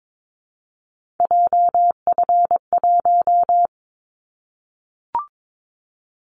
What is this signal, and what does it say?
Morse code, heard as a keyed tone, about 22 wpm, 700 Hz. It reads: JF1
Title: Callsign Prefixes for Top 25 DXCC - 22wpm